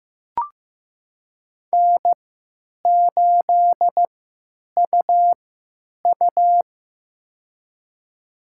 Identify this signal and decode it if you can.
Morse code, heard as a keyed tone, about 15 wpm, spaced as for 9 wpm, 700 Hz. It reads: N8UU
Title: USA Call Signs - 4 Characters - 3x Character Spacing - 15wpm